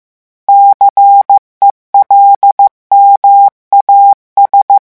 Morse code, heard as a keyed tone, about 15 wpm, 800 Hz. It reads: CELMAS
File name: morse mp3 / revision9 citations humoristiques vitesse 14 mots minute